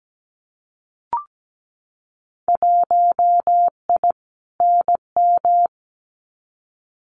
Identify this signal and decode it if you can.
Morse code, heard as a keyed tone, about 17 wpm, 700 Hz. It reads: 1I NM
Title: Winter Field Day Exchanges Encore 17wpm